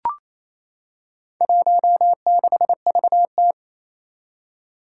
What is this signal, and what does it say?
Morse code, heard as a keyed tone, about 27 wpm, 700 Hz. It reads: 16VT
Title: Four Challenging Letter-Number Combinations 28wpm